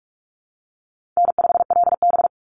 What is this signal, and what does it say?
Morse code, heard as a keyed tone, about 44 wpm, 700 Hz. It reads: N5LB
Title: Easy Call Signs Repeated 3x - 45wpm